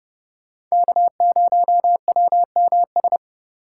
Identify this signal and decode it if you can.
Morse code, heard as a keyed tone, about 30 wpm, 700 Hz. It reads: K0WMS